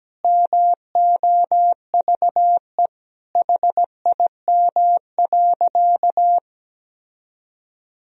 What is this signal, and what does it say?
Morse code, heard as a keyed tone, about 17 wpm, 700 Hz. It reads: MOVE HIM.
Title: Sentences from Top 1000 Words - Review Part 1 - 17wpm